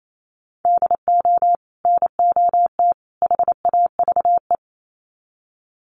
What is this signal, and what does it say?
Morse code, heard as a keyed tone, about 28 wpm, 700 Hz. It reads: DO NOT HAVE
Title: Sets of 3 Words 28wpm